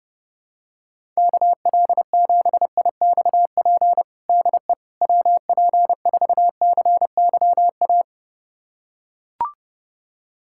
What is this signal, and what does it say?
Morse code, heard as a keyed tone, about 30 wpm, 700 Hz. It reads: KL7IXP DE WP4CYA
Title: USA Callsign DE Callsign - 6 Characters - 30wpm